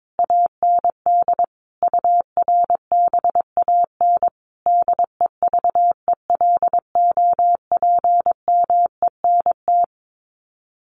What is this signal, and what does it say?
Morse code, heard as a keyed tone, about 22 wpm, 700 Hz. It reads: AND URBAN DEVELOPMENT
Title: Sets of 5 Words Encore2 22wpm